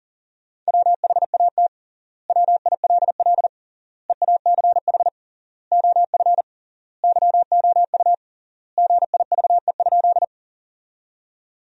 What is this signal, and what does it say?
Morse code, heard as a keyed tone, about 38 wpm, 700 Hz. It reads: WHAT WILL EACH OF YOU GIVE?
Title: 3x Word-Spacing - Sentences from Top 200 Words - 40wpm